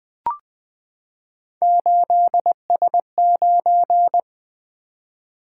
Morse code, heard as a keyed tone, about 20 wpm, 700 Hz. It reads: 8S9
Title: Three Letter-Number Combinations - 20wpm